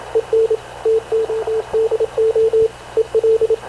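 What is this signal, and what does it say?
Morse code, heard as a keyed tone, about 26 wpm, 440 Hz, with radio noise. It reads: EN TODO EL
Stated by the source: Havana Radio CLA press (PX) broadcast, recorded by Radio Officer Paul Dumesnil aboard mv CNAV QUEST/CZDO.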